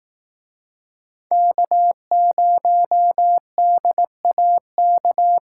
Morse code, heard as a keyed tone, about 18 wpm, 700 Hz. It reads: K0DAK
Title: Speed Racing - USA 1x3 Call Signs 15wpm